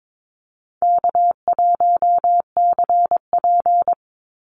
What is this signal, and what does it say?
Morse code, heard as a keyed tone, about 22 wpm, 700 Hz. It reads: K1CP
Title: Easy Call Signs - 22wpm